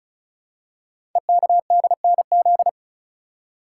Morse code, heard as a keyed tone, about 36 wpm, 690 Hz. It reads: EKDNZ
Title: Code Groups - 5 Letters - 35wpm